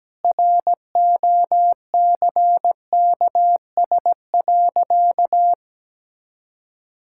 Morse code, heard as a keyed tone, about 17 wpm, 700 Hz. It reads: ROCKS.
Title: Pangram Challenge v2 - 17wpm